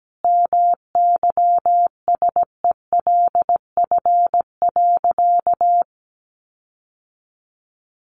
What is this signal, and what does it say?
Morse code, heard as a keyed tone, about 17 wpm, 700 Hz. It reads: MYSELF.
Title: Sentences from Top 1300 Words 17wpm